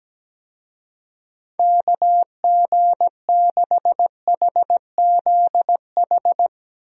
Morse code, heard as a keyed tone, about 17 wpm, 700 Hz. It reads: KG6HZH
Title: USA Call Signs - 6 Characters - 17wpm